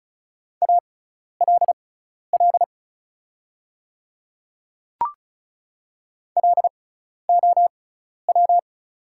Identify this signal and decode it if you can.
Morse code, heard as a keyed tone, about 36 wpm, spaced as for 12 wpm, 690 Hz. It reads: ALL LOW
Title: 6x Character Spacing - Top 300 Words - 35wpm